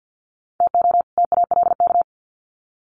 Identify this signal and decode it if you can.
Morse code, heard as a keyed tone, about 50 wpm, 700 Hz. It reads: TO TALK
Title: Sets of 2 Words 50wpm